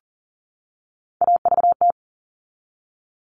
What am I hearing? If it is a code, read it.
Morse code, heard as a keyed tone, about 40 wpm, 700 Hz. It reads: AVT
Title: All Cut Numbers - 3-Digit 40wpm